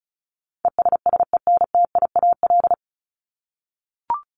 Morse code, heard as a keyed tone, about 34 wpm, 700 Hz. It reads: ESSENTIAL
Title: LY Suffix - Top 100 Words with Base Words 35wpm